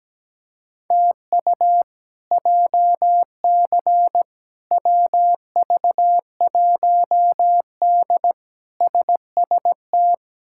Morse code, heard as a keyed tone, about 17 wpm, 700 Hz. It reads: TU JC WV1D SST
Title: K1USN SST QSO Practice - 17wpm